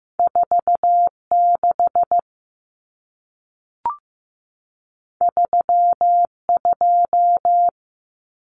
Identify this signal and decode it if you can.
Morse code, heard as a keyed tone, about 15 wpm, 700 Hz. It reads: 46 32
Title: Numbers - 2-Digit 15wpm